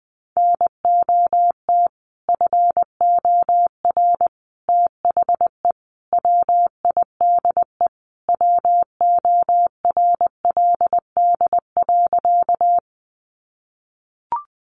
Morse code, heard as a keyed tone, about 20 wpm, 700 Hz. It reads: NOT FOR THE WIDE WORLD.
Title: Sentences from Top 900 Words 20wpm